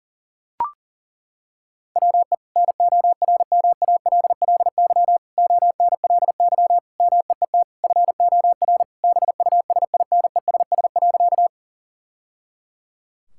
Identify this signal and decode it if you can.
Morse code, heard as a keyed tone, about 37 wpm, 700 Hz. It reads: WE NORMALLY ONLY MEET FOR BUSINESS.
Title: Sentences from Top 1000 Words - Review Part 6 - 40wpm